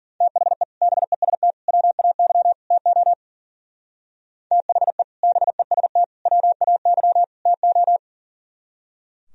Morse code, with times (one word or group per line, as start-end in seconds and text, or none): 0.2-0.6 THE
0.8-1.5 BEST
1.7-2.5 WAY
2.7-3.1 TO
4.5-5.0 THE
5.2-6.0 BEST
6.3-7.2 WAY
7.5-8.0 TO